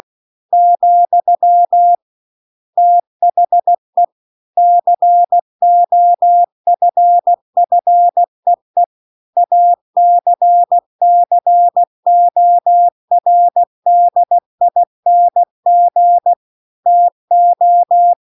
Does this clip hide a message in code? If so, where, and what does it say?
Yes: Morse code from 0.5 to 18.1 s, heard as a keyed tone, about 16 wpm, 700 Hz. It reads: , THE COFFEE ACCORDING TO